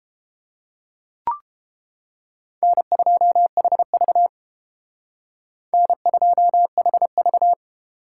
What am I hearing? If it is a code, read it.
Morse code, heard as a keyed tone, about 29 wpm, 700 Hz. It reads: N2HV N2HV